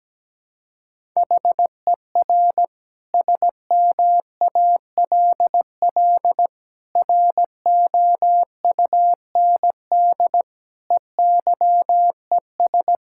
Morse code, heard as a keyed tone, about 17 wpm, 700 Hz. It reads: HER SMALL ROUND EYES
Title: Sentences from Top 1000 Words - Review Part 6 - 17wpm